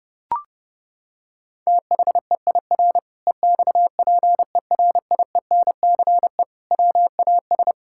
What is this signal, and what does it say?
Morse code, heard as a keyed tone, about 30 wpm, 700 Hz. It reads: THEIR EXPERIENCE WAS
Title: Sentences from Top 1000 Words - Review Part 5 - 30wpm